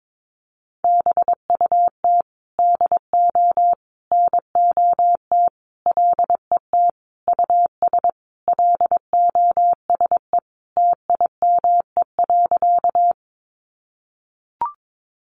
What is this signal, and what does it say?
Morse code, heard as a keyed tone, about 22 wpm, 700 Hz. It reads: BUT DO NOT LET US LOSE TIME.
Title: Sentences from Top 1100 Words 22wpm